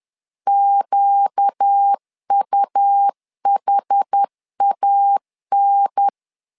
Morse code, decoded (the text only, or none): QUHAN